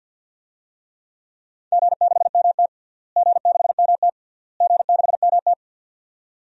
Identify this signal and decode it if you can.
Morse code, heard as a keyed tone, about 46 wpm, 700 Hz. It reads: G6MT G6MT G6MT
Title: DXCC-100 Call Signs - 4 Characters - Repeated 3x - 50wpm